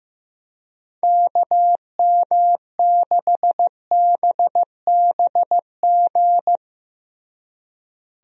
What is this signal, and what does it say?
Morse code, heard as a keyed tone, about 15 wpm, 700 Hz. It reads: KM6BBG
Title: USA Call Signs - 6 Characters - Repeated 2x - 15wpm